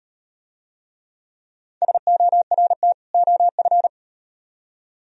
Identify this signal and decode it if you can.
Morse code, heard as a keyed tone, about 35 wpm, 700 Hz. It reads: SORT OF